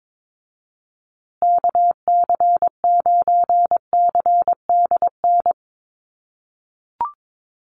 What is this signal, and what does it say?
Morse code, heard as a keyed tone, about 22 wpm, 700 Hz. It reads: KC9CDN
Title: USA Call Signs Repeated 2x 22wpm